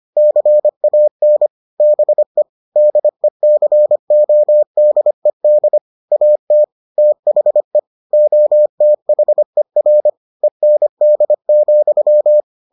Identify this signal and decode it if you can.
Morse code, heard as a keyed tone, about 25 wpm, 600 Hz. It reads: CAN BE DECODED AT THE OTHER END,